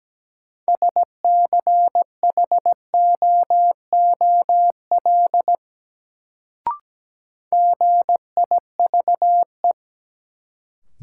Morse code, heard as a keyed tone, about 17 wpm, 700 Hz. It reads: SCHOOL GIVE
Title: Sets of 2 Words - New Word Drill - 17wpm